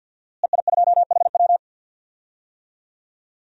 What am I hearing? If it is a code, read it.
Morse code, heard as a keyed tone, about 45 wpm, 700 Hz. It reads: EI2HW
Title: DXCC-100 Call Signs - 5 Characters - Repeated 3x - 50wpm